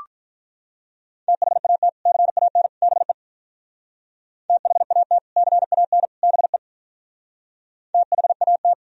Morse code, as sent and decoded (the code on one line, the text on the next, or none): - .... .- - / -.-. .- -. / -... / - .... .- - / -.-. .- -. / -... / - .... .- -
THAT CAN BE THAT CAN BE THAT